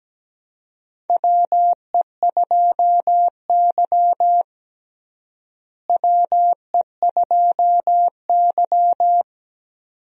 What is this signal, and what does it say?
Morse code, heard as a keyed tone, about 17 wpm, 700 Hz. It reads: WE2Y WE2Y
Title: USA Call Signs - 4 Characters - Repeated 2x - 17wpm